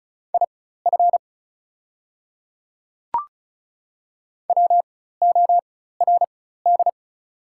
Morse code, heard as a keyed tone, about 36 wpm, spaced as for 16 wpm, 690 Hz. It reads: IF WORD
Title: 4x Character Spacing - Top 200 Words - 35wpm